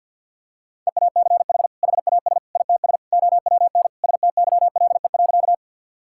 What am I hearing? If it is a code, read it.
Morse code, heard as a keyed tone, about 51 wpm, 700 Hz. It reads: EACH HAS ITS OWN STYLE.
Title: Sentences from Top 1000 Words - Review Part 5 - 50wpm